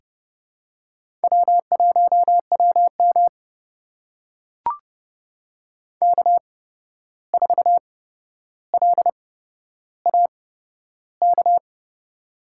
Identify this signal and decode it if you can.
Morse code, heard as a keyed tone, about 30 wpm, spaced as for 12 wpm, 700 Hz. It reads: W1WM K4LAK
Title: USA Call Signs - 8x Character Spacing - 30wpm